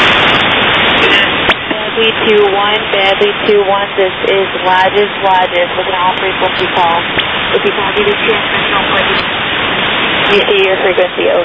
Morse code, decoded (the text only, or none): none